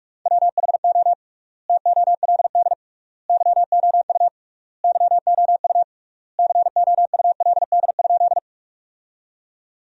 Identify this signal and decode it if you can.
Morse code, heard as a keyed tone, about 46 wpm, 700 Hz. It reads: WHO TOLD YOU YOU COULD?
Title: 3x Word-Spacing - Sentences from Top 400 Words - 45wpm